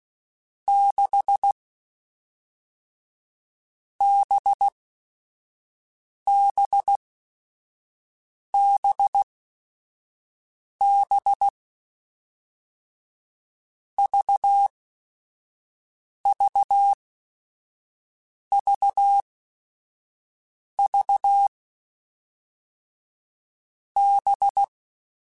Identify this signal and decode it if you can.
Morse code, heard as a keyed tone, about 16 wpm, 800 Hz. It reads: 6 B B B B V V V V B
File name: Groups@5 / Groups@5 mp3s / VB46@5